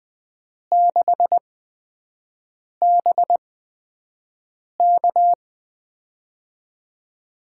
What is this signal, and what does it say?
Morse code, heard as a keyed tone, about 20 wpm, spaced as for 5 wpm, 700 Hz. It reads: 6BK
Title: USA Call Signs - 5 Characters - 8x Character Spacing - 20wpm